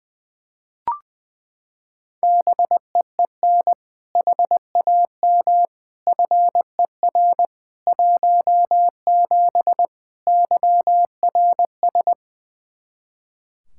Morse code, heard as a keyed tone, about 21 wpm, 700 Hz. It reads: BEEN HAM FER 17 YRS
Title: Practice QSO - Exchange 3 - More Info - 20wpm